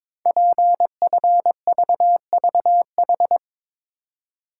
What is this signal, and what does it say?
Morse code, heard as a keyed tone, about 22 wpm, 700 Hz. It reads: PFVVH
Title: Code Groups - 5 Letters - 22wpm